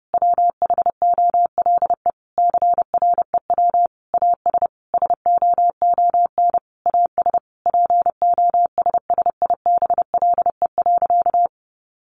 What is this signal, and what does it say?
Morse code, heard as a keyed tone, about 30 wpm, 700 Hz. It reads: WHOLE CREW AS SOON AS POSSIBLE.
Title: Sentences from Top 1500 Words 30wpm